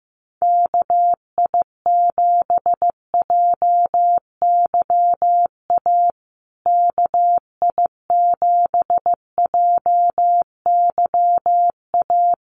Morse code, heard as a keyed tone, about 15 wpm, 700 Hz. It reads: KI7JYA KI7JYA
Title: USA Call Signs Repeated 3x 15wpm